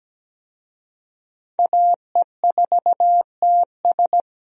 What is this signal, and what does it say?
Morse code, heard as a keyed tone, about 17 wpm, 700 Hz. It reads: AE4TS
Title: USA Call Signs - 5 Characters - 17wpm